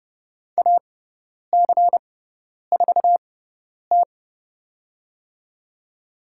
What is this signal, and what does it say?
Morse code, heard as a keyed tone, about 30 wpm, spaced as for 10 wpm, 700 Hz. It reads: AC4T
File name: Lesson 18g - Letter m - Call Signs 30wpm